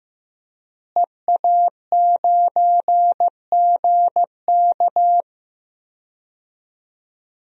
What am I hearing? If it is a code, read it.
Morse code, heard as a keyed tone, about 15 wpm, 700 Hz. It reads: EA9GK
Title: DXCC-100 Call Signs - 5 Characters - Repeated 2x - 15wpm